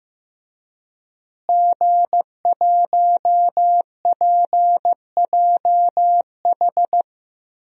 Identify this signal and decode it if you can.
Morse code, heard as a keyed tone, about 15 wpm, 700 Hz. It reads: G1PJH
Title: DXCC-100 Call Signs - 5 Characters - Repeated 3x - 15wpm